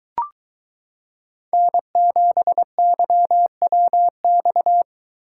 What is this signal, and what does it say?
Morse code, heard as a keyed tone, about 22 wpm, 700 Hz. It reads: N7YWX